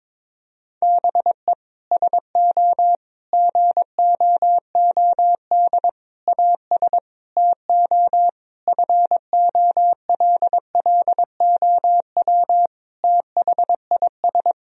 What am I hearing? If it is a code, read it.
Morse code, heard as a keyed tone, about 23 wpm, 700 Hz. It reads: BE SO GOOD AS TO FOLLOW THIS
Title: Sentences from Top 1000 Words - Review Part 1 - 22wpm